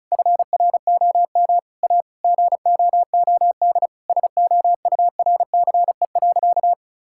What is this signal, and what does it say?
Morse code, heard as a keyed tone, about 36 wpm, 700 Hz. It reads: FROM A GOOD SOURCE.
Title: Sentences from Top 1000 Words - Review Part 5 - 35wpm